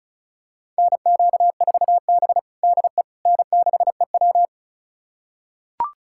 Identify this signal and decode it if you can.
Morse code, heard as a keyed tone, about 34 wpm, 700 Hz. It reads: NQ4B DE N6EW
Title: USA Callsign DE Callsign - 4 Characters - 35wpm